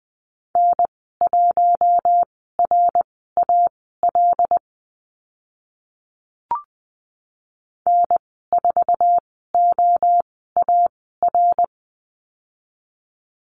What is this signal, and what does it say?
Morse code, heard as a keyed tone, about 20 wpm, spaced as for 15 wpm, 700 Hz. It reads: N1RAL N4OAR
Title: 076 Lesson 13g - Letter L - Call Signs 20wpm